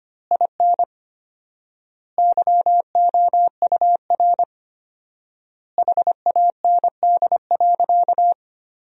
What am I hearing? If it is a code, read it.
Morse code, heard as a keyed tone, about 25 wpm, 700 Hz. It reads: IN YOUR HAND.
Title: 4x Word-Spacing - Sentences from Top 400 Words - 25wpm